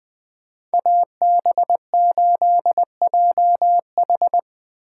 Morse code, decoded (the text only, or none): AB8JH